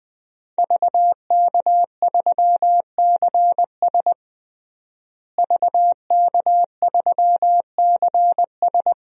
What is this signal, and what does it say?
Morse code, heard as a keyed tone, about 20 wpm, 700 Hz. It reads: VK3CS VK3CS
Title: DXCC-100 Call Signs - 5 Characters - Repeated 2x - 20wpm